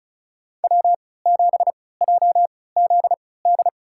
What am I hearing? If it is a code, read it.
Morse code, heard as a keyed tone, about 36 wpm, spaced as for 20 wpm, 700 Hz. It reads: W7JZD
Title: USA Call Signs - 5 Characters - 3x Character Spacing - 35wpm